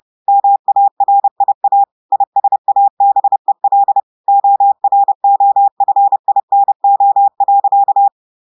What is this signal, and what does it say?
Morse code, heard as a keyed tone, about 30 wpm, 800 Hz. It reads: MARIA ISABEL OROFINO.
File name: audios / portugais / portugais texte en clair 1 vitesse 30 mots minutes